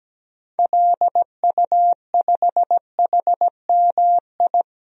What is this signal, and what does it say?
Morse code, heard as a keyed tone, about 17 wpm, 700 Hz. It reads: LU5HMI
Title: DXCC-100 Call Signs - 6 Characters - 17wpm